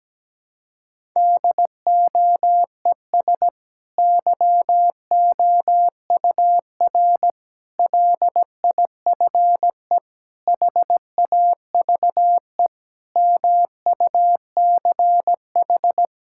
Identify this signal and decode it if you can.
Morse code, heard as a keyed tone, about 17 wpm, 700 Hz. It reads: DOES YOUR LIFE HAVE MUCH
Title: Sentences from Top 1000 Words - Review Part 6 - 17wpm